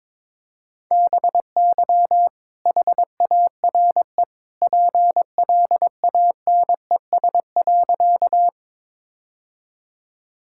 Morse code, heard as a keyed tone, about 22 wpm, 700 Hz. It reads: BY HARE PLANES.
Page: morse-code-ninja-practice-22wpm